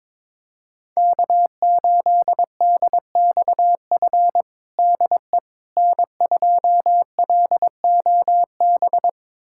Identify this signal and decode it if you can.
Morse code, heard as a keyed tone, about 22 wpm, 700 Hz. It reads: K8DXF DE N2LOB
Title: USA Callsign DE Callsign - 5 Characters - 22wpm